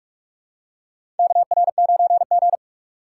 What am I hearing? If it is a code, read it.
Morse code, heard as a keyed tone, about 46 wpm, 700 Hz. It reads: KR9G
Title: USA Call Signs - 4 Characters - 2x Character Spacing - 45wpm